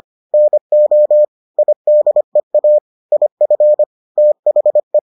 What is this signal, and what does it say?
Morse code, heard as a keyed tone, about 25 wpm, 600 Hz. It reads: NO IDEA IF THE